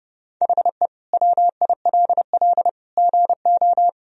Morse code, heard as a keyed tone, about 30 wpm, 700 Hz. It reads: HE WILL GO
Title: Sentences from Top 1000 Words - Review Part 2 - 30wpm